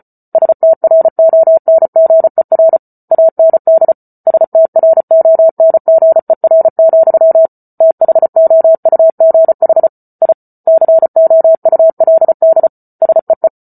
Morse code, heard as a keyed tone, about 34 wpm, 650 Hz. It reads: STRONGER AND STRONGER, THOUGH I COULD SEE
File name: Book Two / 08-Chapter 8 - 34 WPM